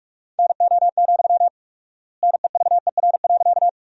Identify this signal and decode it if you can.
Morse code, heard as a keyed tone, about 46 wpm, 700 Hz. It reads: NO, NEVER.
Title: 4x Word-Spacing - Sentences from Top 300 Words - 45wpm